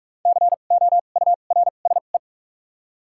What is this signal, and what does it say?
Morse code, heard as a keyed tone, about 47 wpm, spaced as for 33 wpm, 700 Hz. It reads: COURSE
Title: 2x Character Spacing - Top 500 Words - 45wpm